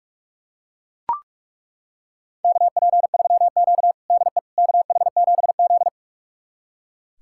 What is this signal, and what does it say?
Morse code, heard as a keyed tone, about 42 wpm, 700 Hz. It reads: KP3Q DE KH7Z
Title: USA Callsign DE Callsign - 4 Characters - 45wpm